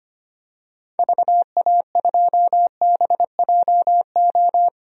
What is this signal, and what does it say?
Morse code, heard as a keyed tone, about 25 wpm, 700 Hz. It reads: VA2BJO
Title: DXCC-100 Call Signs - 6 Characters - 25wpm